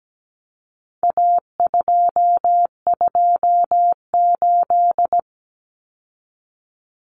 Morse code, heard as a keyed tone, about 17 wpm, 700 Hz. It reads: A228
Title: Common Cut Numbers - 4-Digit 17wpm